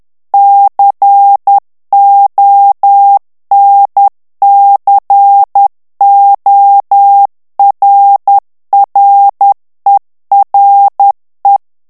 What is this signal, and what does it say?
Morse code, heard as a keyed tone, about 10 wpm, 800 Hz. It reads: CONCORRERE
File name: ITALIEN texte clair 2 vitesse 10 mots minute